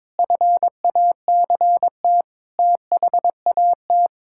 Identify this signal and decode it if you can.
Morse code, heard as a keyed tone, about 22 wpm, 700 Hz. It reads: FACT THAT
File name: practice / 22wpm / Sets of 7 Words - 22wpm